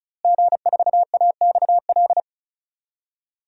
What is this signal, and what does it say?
Morse code, heard as a keyed tone, about 34 wpm, 700 Hz. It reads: G4AXL
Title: DXCC-100 Call Signs - 5 Characters - Repeated 3x - 35wpm